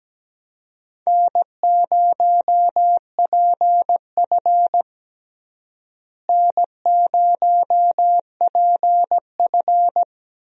Morse code, heard as a keyed tone, about 17 wpm, 700 Hz. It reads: N0PF N0PF
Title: USA Call Signs - 4 Characters - Repeated 2x - 17wpm